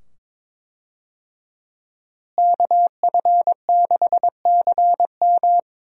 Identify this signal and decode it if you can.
Morse code, heard as a keyed tone, about 22 wpm, 700 Hz. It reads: KF6CM
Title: USA Call Signs - 5 Characters - Repeated 2x - 22wpm